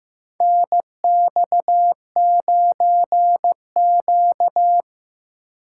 Morse code, heard as a keyed tone, about 15 wpm, 700 Hz. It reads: NX9Q
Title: USA Call Signs - 4 Characters - 2x Character Spacing - 15wpm